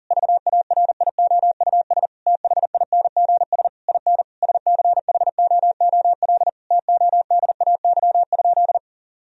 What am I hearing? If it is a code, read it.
Morse code, heard as a keyed tone, about 40 wpm, 700 Hz. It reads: VARIOUS THINGS IN SCHOOL TODAY?
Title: Sentences from Top 1000 Words - Review Part 4 - 40wpm